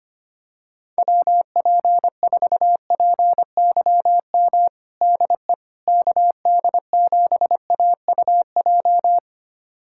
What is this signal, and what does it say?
Morse code, heard as a keyed tone, about 25 wpm, 700 Hz. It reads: WP4PYM DE KD7AUJ
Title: USA Callsign DE Callsign - 6 Characters - 25wpm